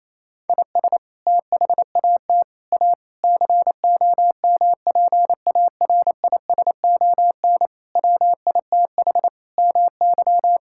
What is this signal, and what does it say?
Morse code, heard as a keyed tone, about 28 wpm, 700 Hz. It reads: IS THAT A COMPARISON WITH MY
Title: Sentences from Top 1600 Words 28wpm